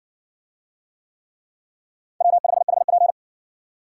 Morse code, heard as a keyed tone, about 60 wpm, 700 Hz. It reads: W5HP